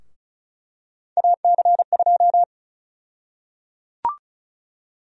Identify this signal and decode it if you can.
Morse code, heard as a keyed tone, about 36 wpm, 700 Hz. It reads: AC2
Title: USA Call Sign Prefixes - Repeated 2x - 35wpm